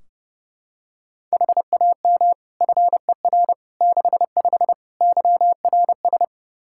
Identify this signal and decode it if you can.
Morse code, heard as a keyed tone, about 30 wpm, 700 Hz. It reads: HAM FER 65 YRS
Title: Practice QSO - Exchange 3 - More Info - 30wpm